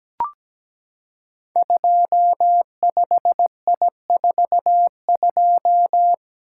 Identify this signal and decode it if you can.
Morse code, heard as a keyed tone, about 17 wpm, 700 Hz. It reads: 25I42
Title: Code Groups - 5 Letter:Numbers - 17wpm